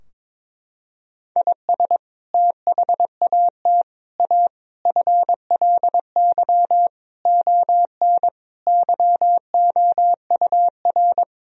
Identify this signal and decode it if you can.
Morse code, heard as a keyed tone, about 22 wpm, 700 Hz. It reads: IS THAT A FLY ON YOUR
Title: Sentences from Top 1000 Words - Review Part 2 - 22wpm